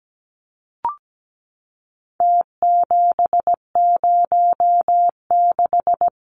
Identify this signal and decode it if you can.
Morse code, heard as a keyed tone, about 17 wpm, 700 Hz. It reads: T706
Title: Numbers - Contest Style - 4-Digit 17wpm